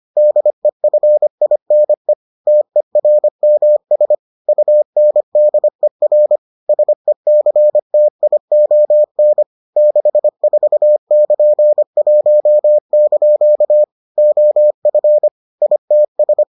Morse code, heard as a keyed tone, about 25 wpm, 600 Hz. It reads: DEFINE TERMS UNDER SECTION 64<KN>1) OF ITS